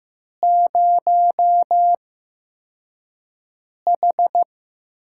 Morse code, heard as a keyed tone, about 15 wpm, spaced as for 4 wpm, 700 Hz. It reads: 0H
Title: USA Call Signs - 4 Characters - 8x Character Spacing - 15wpm